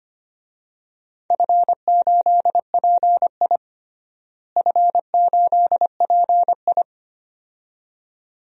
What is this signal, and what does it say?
Morse code, heard as a keyed tone, about 25 wpm, 700 Hz. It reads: F8PI F8PI
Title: DXCC-100 Call Signs - 4 Characters - Repeated 2x - 25wpm